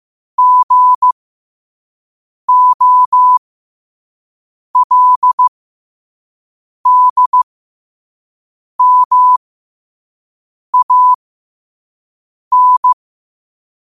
Morse code, heard as a keyed tone, about 15 wpm, spaced as for 5 wpm, 1000 Hz. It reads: GOLDMAN
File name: qotdfiles / 5wpm